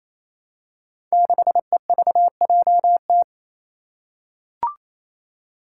Morse code, heard as a keyed tone, about 27 wpm, 700 Hz. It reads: 6EVJT
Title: Five Challenging Letter-Number Combinations 28wpm